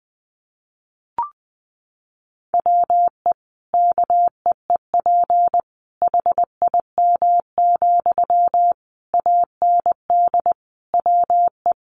Morse code, heard as a keyed tone, about 19 wpm, 700 Hz. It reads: WE KEEP HIM, AND WE
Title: Sentences from Top 600 Words 20wpm